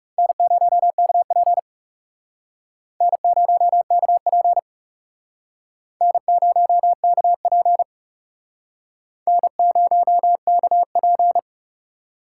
Morse code, heard as a keyed tone, about 37 wpm, 700 Hz. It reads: N0KP N0KP N0KP N0KP
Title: Speed Racing - USA 1x2 Call Signs 30wpm